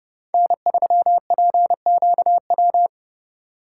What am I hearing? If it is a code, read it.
Morse code, heard as a keyed tone, about 30 wpm, 700 Hz. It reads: N3PQW